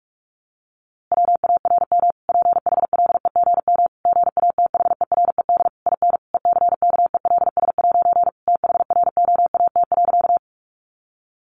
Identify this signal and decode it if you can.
Morse code, heard as a keyed tone, about 44 wpm, 700 Hz. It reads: WARM PHLEGM GATHERED IN ECKELS' THROAT.
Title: A Sound Of Thunder 45wpm